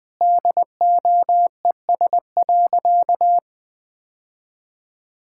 Morse code, heard as a keyed tone, about 20 wpm, 700 Hz. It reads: DOES.
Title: Sentences with 5-Word Phrases - Spoken Beginning - 20wpm